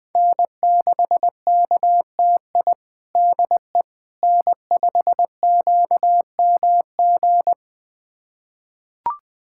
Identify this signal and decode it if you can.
Morse code, heard as a keyed tone, about 20 wpm, 700 Hz. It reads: N6KTI DE N5QMG
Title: USA Callsign DE Callsign - 5 Characters - 20wpm